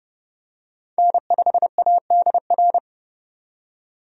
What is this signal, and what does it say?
Morse code, heard as a keyed tone, about 30 wpm, 700 Hz. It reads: N5ADR
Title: USA Call Signs - 5 Characters - 30wpm